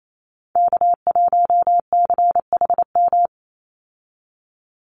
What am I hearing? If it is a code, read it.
Morse code, heard as a keyed tone, about 28 wpm, 700 Hz. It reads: K1CHM
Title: Call Signs Repeated 3x - 28wpm